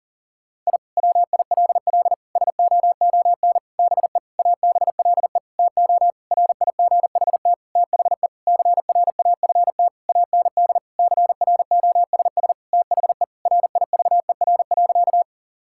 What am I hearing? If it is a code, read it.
Morse code, heard as a keyed tone, about 40 wpm, 700 Hz. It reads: I WILL SOON BE ABLE TO RIGHT THE CRAFT AND CROSS THE RIVER.
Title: Sentences from Top 1000 Words - Review Part 6 - 40wpm